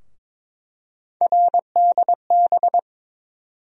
Morse code, heard as a keyed tone, about 22 wpm, 700 Hz. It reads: RDB
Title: USA Call Sign Postfixes - 3 Letters 22wpm